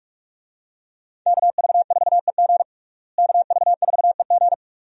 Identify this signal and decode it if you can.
Morse code, heard as a keyed tone, about 46 wpm, 700 Hz. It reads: KV4EG KV4EG
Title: USA Call Signs - 5 Characters - Repeated 3x - 45wpm